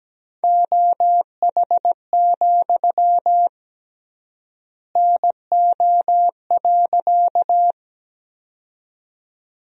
Morse code, heard as a keyed tone, about 17 wpm, 700 Hz. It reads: OH, NO.
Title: 3x Word-Spacing - Sentences from Top 500 Words - 17wpm